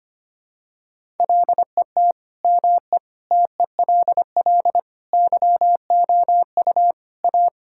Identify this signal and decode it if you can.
Morse code, heard as a keyed tone, about 25 wpm, 700 Hz. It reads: LET ME TELL YOU A
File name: practice / 25wpm / Sentences with 4-Word Phrases - 25wpm